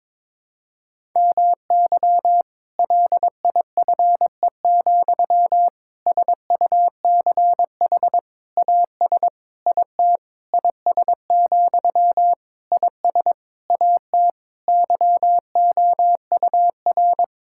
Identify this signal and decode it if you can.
Morse code, heard as a keyed tone, about 22 wpm, 700 Hz. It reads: MY LIFE, SUCH AS IT IS, IS AT YOUR
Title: Sentences from Top 1600 Words 22wpm